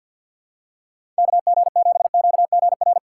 Morse code, heard as a keyed tone, about 51 wpm, 700 Hz. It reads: KG7QGR
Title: USA Call Signs - 6 Characters - 8x Character Spacing - 50wpm